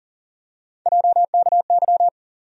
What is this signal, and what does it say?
Morse code, heard as a keyed tone, about 40 wpm, 700 Hz. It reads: JKY
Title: USA Call Sign Postfixes - 3 Letters 40wpm